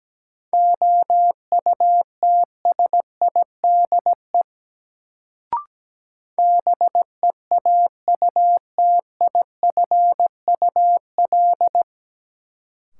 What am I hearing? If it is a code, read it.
Morse code, heard as a keyed tone, about 17 wpm, 700 Hz. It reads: OUTSIDE BEAUTIFUL
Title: Sentences from Top 900 Words - New Word Drill - 17wpm